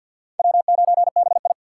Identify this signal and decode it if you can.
Morse code, heard as a keyed tone, about 47 wpm, 700 Hz. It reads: W9BI
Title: USA Call Signs - 4x Character Spacing - 50wpm